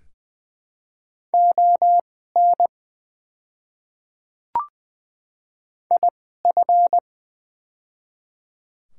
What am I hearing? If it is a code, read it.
Morse code, heard as a keyed tone, about 20 wpm, spaced as for 15 wpm, 700 Hz. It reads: ON IF
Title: 2x Character Spacing - Top 200 Words - 20wpm